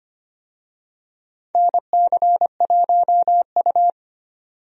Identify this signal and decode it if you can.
Morse code, heard as a keyed tone, about 25 wpm, 700 Hz. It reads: NC1U
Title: USA Call Signs - 4 Characters - 25wpm